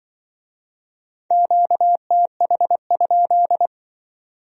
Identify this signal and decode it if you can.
Morse code, heard as a keyed tone, about 25 wpm, 700 Hz. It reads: QTH?
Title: Speed-Racing Q-Codes - 20wpm